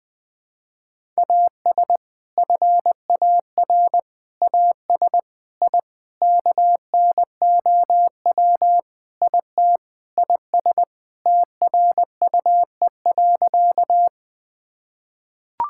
Morse code, heard as a keyed tone, about 20 wpm, 700 Hz. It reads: AS FAR AS I KNOW IT IS TRUE.
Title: Sentences with 5-Word Phrases - 20wpm